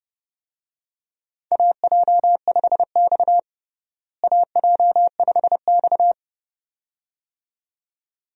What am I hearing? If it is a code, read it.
Morse code, heard as a keyed tone, about 30 wpm, 700 Hz. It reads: AJ5X AJ5X
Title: USA Call Signs - 4 Characters - Repeated 2x - 30wpm